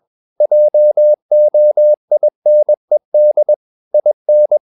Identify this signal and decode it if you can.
Morse code, heard as a keyed tone, about 21 wpm, 600 Hz. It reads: JOINED IN
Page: CW med Gnister